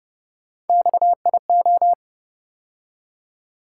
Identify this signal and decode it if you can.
Morse code, heard as a keyed tone, about 32 wpm, 700 Hz. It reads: XIO